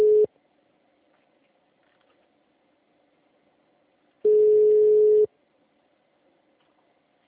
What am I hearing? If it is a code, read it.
Ringback tone (Europe, 425 Hz).
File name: out- -2003- - - .2147